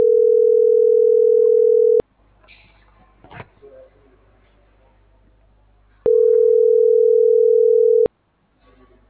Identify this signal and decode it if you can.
Ringback tone (North America), 440 + 480 Hz.